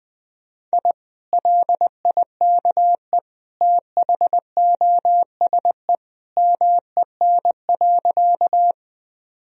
Morse code, heard as a keyed tone, about 21 wpm, 700 Hz. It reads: I LIKE THOSE MEN.
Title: Sentences from Top 1000 Words - Review Part 1 - 20wpm